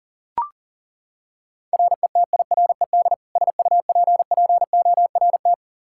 Morse code, heard as a keyed tone, about 40 wpm, 690 Hz. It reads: RETIRED SUPPORT
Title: Practice QSO - Exchange 3 - More Info - 40wpm